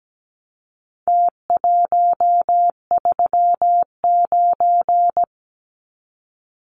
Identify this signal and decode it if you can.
Morse code, heard as a keyed tone, about 17 wpm, 700 Hz. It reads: T139
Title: Numbers - Contest Style - 4-Digit 17wpm